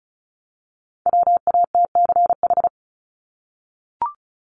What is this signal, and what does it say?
Morse code, heard as a keyed tone, about 34 wpm, 700 Hz. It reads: WATCH
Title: Top 500 Words 35wpm